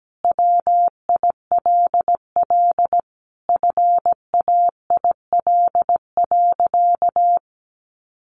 Morse code, heard as a keyed tone, about 17 wpm, 700 Hz. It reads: WILL FAIL.
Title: Sentences from Top 1500 Words 17wpm